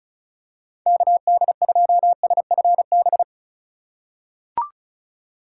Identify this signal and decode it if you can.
Morse code, heard as a keyed tone, about 36 wpm, 700 Hz. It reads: KD2SFB
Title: USA Call Signs - 3x Character Spacing - 35wpm